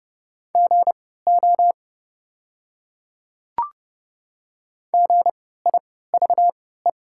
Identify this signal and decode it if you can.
Morse code, heard as a keyed tone, about 31 wpm, spaced as for 17 wpm, 700 Hz. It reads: GO GIVE
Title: 3x Character Spacing - Top 200 Words - 30wpm